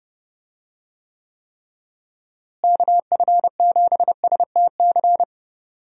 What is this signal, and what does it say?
Morse code, heard as a keyed tone, about 30 wpm, 700 Hz. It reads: KF7STC
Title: USA Call Signs - 6 Characters - Repeated 3x - 30wpm